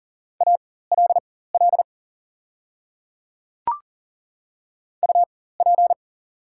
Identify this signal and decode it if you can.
Morse code, heard as a keyed tone, about 40 wpm, spaced as for 19 wpm, 700 Hz. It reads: ALL UP
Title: 4x Character Spacing - Top 100 Words - 40wpm